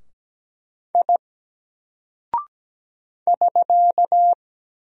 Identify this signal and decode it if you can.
Morse code, heard as a keyed tone, about 17 wpm, 700 Hz. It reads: I <SK>
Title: Single Character (Letter:Number:Punctuation:Prosign) Alt 17wpm